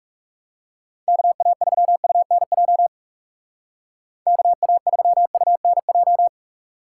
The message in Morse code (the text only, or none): KA3UNJ KA3UNJ